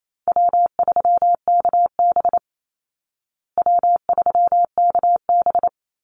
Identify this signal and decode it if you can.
Morse code, heard as a keyed tone, about 28 wpm, 700 Hz. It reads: W3KB W3KB
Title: Easy Call Signs Repeated 2x - 28wpm